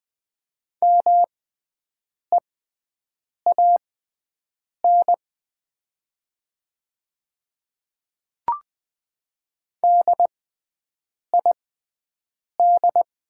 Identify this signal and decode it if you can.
Morse code, heard as a keyed tone, about 20 wpm, spaced as for 7 wpm, 700 Hz. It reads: MEAN DID
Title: 6x Character Spacing - Top 200 Words - 20wpm